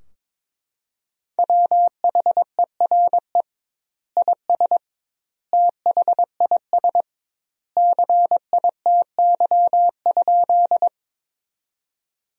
Morse code, heard as a keyed tone, about 22 wpm, 700 Hz. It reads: WHERE IS THIS CITY?
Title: 2x Word-Spacing - Sentences from Top 300 Words - 22wpm